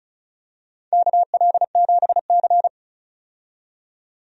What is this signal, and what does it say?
Morse code, heard as a keyed tone, about 36 wpm, 700 Hz. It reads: KL7C
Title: DXCC-100 Call Signs - 4 Characters - 35wpm